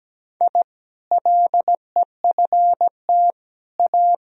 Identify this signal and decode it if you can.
Morse code, heard as a keyed tone, about 17 wpm, 700 Hz. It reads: I LEFT A
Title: Sentences from Top 1000 Words - Review Part 4 - 17wpm